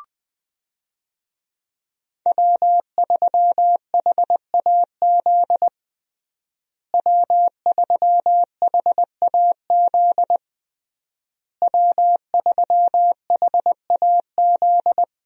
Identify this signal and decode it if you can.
Morse code, heard as a keyed tone, about 20 wpm, 700 Hz. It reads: W3HAZ W3HAZ W3HAZ
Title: USA Call Signs - 5 Characters - Repeated 3x - 20wpm